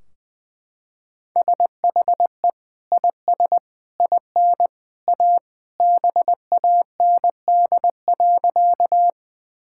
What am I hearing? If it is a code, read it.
Morse code, heard as a keyed tone, about 20 wpm, 700 Hz. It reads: SHE IS IN A BAND.
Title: Sentences from Top 1600 Words 20wpm